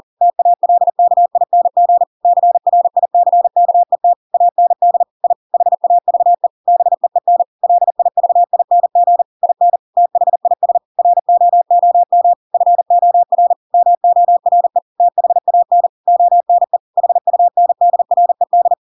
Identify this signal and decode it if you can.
Morse code, heard as a keyed tone, about 40 wpm, 700 Hz. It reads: TALKING CRICKET AND I HAVE BEEN LIVING IN THIS ROOM FOR MORE THAN ONE HUNDRED
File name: Pinocchio / Pinocchio ch4 40 40 ik1hge